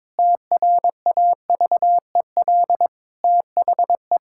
Morse code, heard as a keyed tone, about 22 wpm, 700 Hz. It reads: TRAVEL THE
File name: Difficult Sentences with Hints - 22wpm